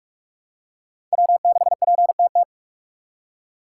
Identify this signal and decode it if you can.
Morse code, heard as a keyed tone, about 44 wpm, 700 Hz. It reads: W6PTT
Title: USA Call Signs - 5 Characters - 4x Character Spacing - 45wpm